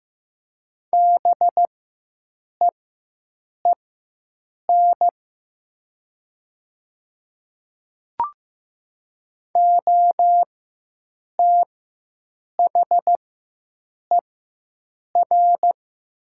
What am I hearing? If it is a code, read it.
Morse code, heard as a keyed tone, about 15 wpm, spaced as for 7 wpm, 700 Hz. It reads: BEEN OTHER
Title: 4x Character Spacing - Top 100 Words - 15wpm